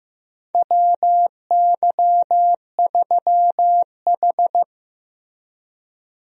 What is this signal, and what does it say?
Morse code, heard as a keyed tone, about 16 wpm, 700 Hz. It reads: WY3H
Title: USA Call Signs - 4 Characters - Repeated 2x - 15wpm